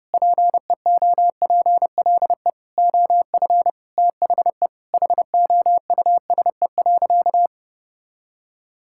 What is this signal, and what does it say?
Morse code, heard as a keyed tone, about 32 wpm, 700 Hz. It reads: PEOPLE OF THE HOUSE.
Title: Sentences from Top 1000 Words - Review Part 5 - 30wpm